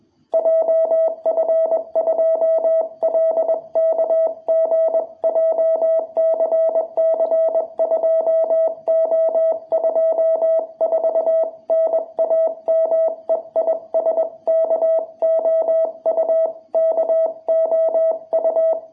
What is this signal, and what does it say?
Morse code, heard as a keyed tone, about 21 wpm, 660 Hz. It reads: JF2LKG JCC2O24 NAME IS KOU KOU
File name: jh8cbh / 2021cw